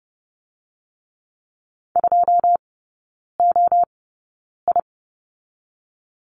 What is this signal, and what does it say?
Morse code, heard as a keyed tone, about 30 wpm, 700 Hz. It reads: 2 O I
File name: Lesson 19e - Letter w - Three Characters 30wpm